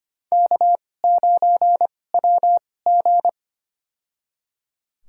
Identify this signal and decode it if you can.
Morse code, heard as a keyed tone, about 25 wpm, spaced as for 18 wpm, 700 Hz. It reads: K9WG
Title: USA Call Signs - 2x Character Spacing - 25wpm